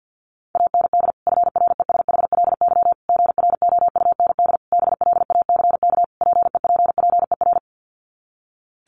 Morse code, heard as a keyed tone, about 50 wpm, 700 Hz. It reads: AND FRESHLY GROUND BLACK PEPPER
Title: Sets of 5 Words Encore 50wpm